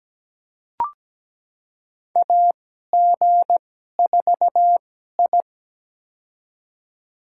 Morse code, heard as a keyed tone, about 17 wpm, spaced as for 12 wpm, 700 Hz. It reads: AG4I
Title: USA Call Signs - 4 Characters - 2x Character Spacing - 17wpm